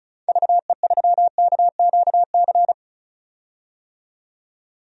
Morse code, heard as a keyed tone, about 36 wpm, 700 Hz. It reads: VE3KQC
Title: DXCC-100 Call Signs - 6 Characters - 35wpm